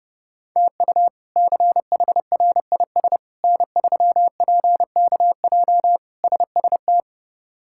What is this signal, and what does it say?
Morse code, heard as a keyed tone, about 30 wpm, 700 Hz. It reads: TU CHRIS N3PKJ SST
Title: K1USN SST QSO Practice - 30wpm